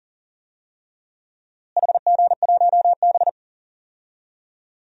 Morse code, heard as a keyed tone, about 40 wpm, 700 Hz. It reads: HG1B
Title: DXCC-100 Call Signs - 4 Characters - Repeated 3x - 40wpm